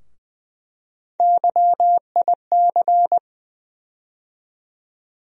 Morse code, heard as a keyed tone, about 20 wpm, 700 Hz. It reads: YIC
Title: USA Call Sign Postfixes - 3 Letters 20wpm